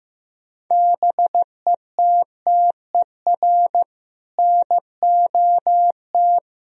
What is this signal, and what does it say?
Morse code, heard as a keyed tone, about 16 wpm, 700 Hz. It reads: BETTER NOT
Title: Sentences from Top 1000 Words - Review Part 4 - 15wpm